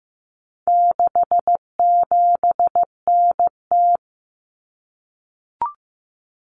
Morse code, heard as a keyed tone, about 15 wpm, 700 Hz. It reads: 67NT
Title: Common Cut Numbers - 4-Digit 15wpm